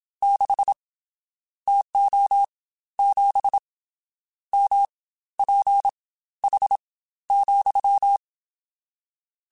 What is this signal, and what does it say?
Morse code, heard as a keyed tone, about 27 wpm, spaced as for 14 wpm, 800 Hz. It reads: 6 TO 7 MPH,
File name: Spaced Characters / Spaced Characters-mp3s / Weather4 24 sp @25